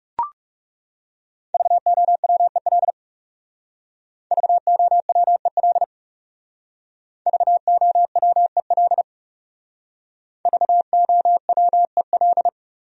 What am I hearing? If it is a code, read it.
Morse code, heard as a keyed tone, about 39 wpm, 700 Hz. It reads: VOWEL VOWEL VOWEL VOWEL